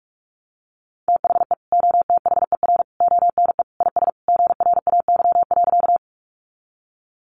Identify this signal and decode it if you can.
Morse code, heard as a keyed tone, about 44 wpm, 700 Hz. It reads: THE OTHER ONE IS GRAY.
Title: Additional Easy Phrases 45wpm